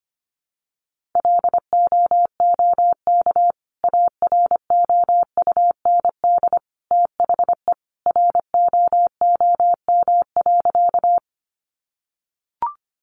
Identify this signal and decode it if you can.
Morse code, heard as a keyed tone, about 25 wpm, 700 Hz. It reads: LOOK AROUND THE ROOM.
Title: Sentences from Top 600 Words 25wpm